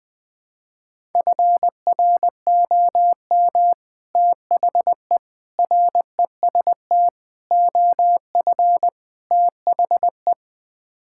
Morse code, heard as a keyed tone, about 20 wpm, 700 Hz. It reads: FROM THE REST OF THE
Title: Build A Phrase - 5-Word Phrases - 20wpm